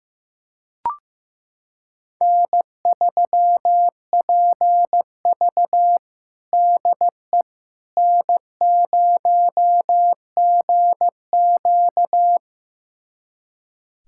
Morse code, heard as a keyed tone, about 15 wpm, 700 Hz. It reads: N3PV DE N0GQ
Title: USA Callsign DE Callsign - 4 Characters - 15wpm